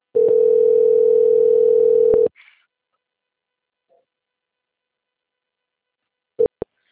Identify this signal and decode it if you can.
Ringback tone (North America), 440 + 480 Hz.
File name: out- - - - - .28859